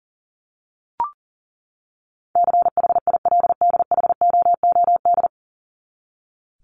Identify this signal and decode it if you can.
Morse code, heard as a keyed tone, about 37 wpm, 700 Hz. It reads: CHILDHOOD
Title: Top 1751-2000 Words 40wpm